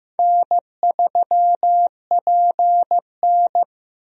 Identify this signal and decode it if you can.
Morse code, heard as a keyed tone, about 15 wpm, 700 Hz. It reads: N3PN
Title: USA Call Signs - 4 Characters - 15wpm